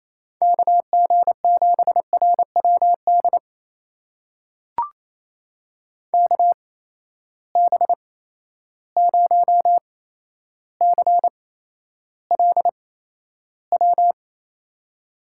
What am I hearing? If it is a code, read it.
Morse code, heard as a keyed tone, about 28 wpm, spaced as for 12 wpm, 700 Hz. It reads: KG7RWD KB0CLW
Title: USA Call Signs - 6 Characters - 8x Character Spacing - 28wpm